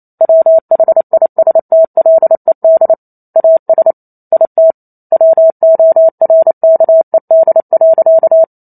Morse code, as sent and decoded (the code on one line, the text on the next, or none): .-- .... .. ... - .-.. -.. / .- ... / .. - / .-- --- .-. -.- . -.. .-.-.-
WHISTLED AS IT WORKED.